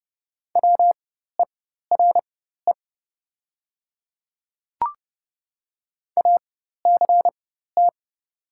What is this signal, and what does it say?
Morse code, heard as a keyed tone, about 30 wpm, spaced as for 14 wpm, 700 Hz. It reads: WERE ACT
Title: 4x Character Spacing - Top 200 Words - 30wpm